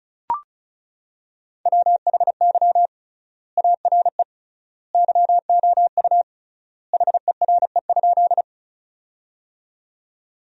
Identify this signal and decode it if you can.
Morse code, heard as a keyed tone, about 34 wpm, 700 Hz. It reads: WHY ARE YOU HERE?
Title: 3x Word-Spacing - Sentences from Top 200 Words - 35wpm